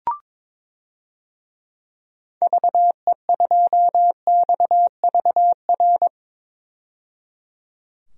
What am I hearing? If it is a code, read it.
Morse code, heard as a keyed tone, about 22 wpm, 700 Hz. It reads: VE2XVR
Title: DXCC-100 Call Signs - 6 Characters - 22wpm